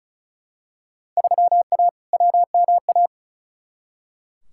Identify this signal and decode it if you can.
Morse code, heard as a keyed tone, about 36 wpm, 700 Hz. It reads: 3A WMA
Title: Field Day Exchanges Encore - 35wpm